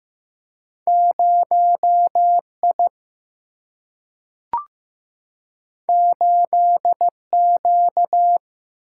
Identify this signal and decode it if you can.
Morse code, heard as a keyed tone, about 15 wpm, 700 Hz. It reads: 0I 8Q
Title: Single-Letter Single-Number 15wpm